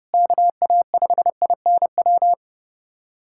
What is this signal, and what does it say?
Morse code, heard as a keyed tone, about 30 wpm, 700 Hz. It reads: KA5INW
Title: USA Callsign DE Callsign - 6 Characters - 30wpm